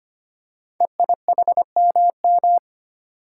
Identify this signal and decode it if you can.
Morse code, heard as a keyed tone, about 25 wpm, 700 Hz. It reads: EIHMM
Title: Code Groups - 5 Letters - 25wpm